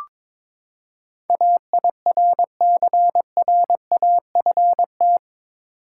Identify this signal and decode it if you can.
Morse code, heard as a keyed tone, about 22 wpm, 700 Hz. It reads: AIRCRAFT
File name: practice / 22wpm / Eight Letter Words - 22wpm